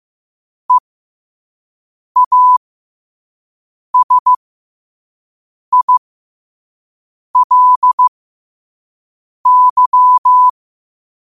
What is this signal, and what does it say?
Morse code, heard as a keyed tone, about 15 wpm, spaced as for 5 wpm, 1000 Hz. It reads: EASILY
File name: qotdfiles / 5wpm